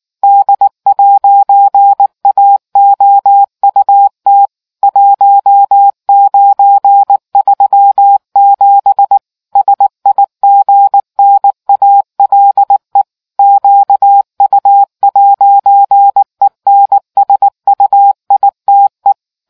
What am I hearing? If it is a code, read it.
Morse code, heard as a keyed tone, about 19 wpm, 800 Hz. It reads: D'AOUT 1937 SIGNALE QU'ENSUITE